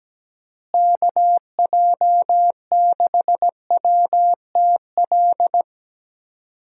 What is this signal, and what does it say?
Morse code, heard as a keyed tone, about 17 wpm, 700 Hz. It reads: KJ6WTL
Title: USA Call Signs - 6 Characters - Repeated 3x - 17wpm